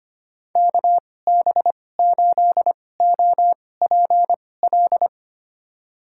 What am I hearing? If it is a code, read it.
Morse code, heard as a keyed tone, about 25 wpm, spaced as for 18 wpm, 700 Hz. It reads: KB8OPL
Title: USA Call Signs - 6 Characters - 2x Character Spacing - 25wpm